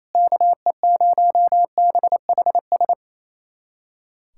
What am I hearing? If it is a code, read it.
Morse code, heard as a keyed tone, about 28 wpm, 700 Hz. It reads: KE0BHS
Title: USA Callsign DE Callsign - 6 Characters - 28wpm